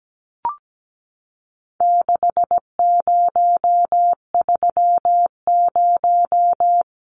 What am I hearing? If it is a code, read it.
Morse code, heard as a keyed tone, about 17 wpm, 700 Hz. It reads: 6030
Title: Numbers - 4-Digit 17wpm